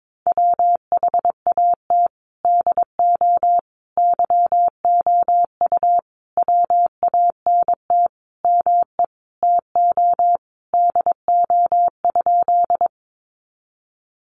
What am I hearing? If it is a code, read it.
Morse code, heard as a keyed tone, about 22 wpm, 700 Hz. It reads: WHAT DO YOU WANT ME TO DO?
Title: Sentences from Top 200 Words 22wpm